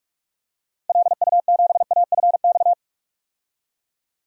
Morse code, heard as a keyed tone, about 46 wpm, 700 Hz. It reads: LU7AFX
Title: DXCC-100 Call Signs - 6 Characters - 45wpm